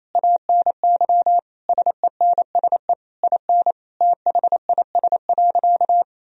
Morse code, heard as a keyed tone, about 28 wpm, 700 Hz. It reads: ANY SENSE IN THIS.
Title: Sentences from Top 1000 Words - Review Part 4 - 28wpm